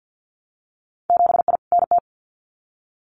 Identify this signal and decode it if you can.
Morse code, heard as a keyed tone, about 50 wpm, 700 Hz. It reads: 7I NT
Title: Winter Field Day Exchanges Encore 50wpm